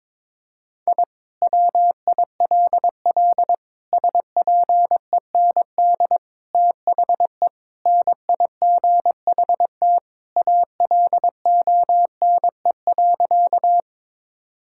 Morse code, heard as a keyed tone, about 22 wpm, 700 Hz. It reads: I WILL SPEND THE NIGHT ALONE.
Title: Sentences from Top 1000 Words - Review Part 5 - 22wpm